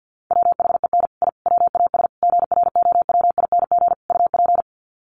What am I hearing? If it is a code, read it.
Morse code, heard as a keyed tone, about 49 wpm, 700 Hz. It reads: WHEN I WAS GROWING UP
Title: Sets of 5 Words Encore 50wpm